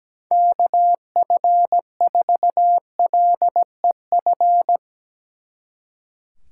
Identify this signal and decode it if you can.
Morse code, heard as a keyed tone, about 19 wpm, 700 Hz. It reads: KF4LEF
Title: Speed Racing - USA 2x3 Call Signs 17wpm